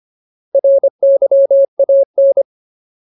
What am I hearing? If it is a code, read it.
Morse code, heard as a keyed tone, about 25 wpm, 550 Hz. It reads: RYAN
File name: Ultimate Word List / 25WPM Ultimate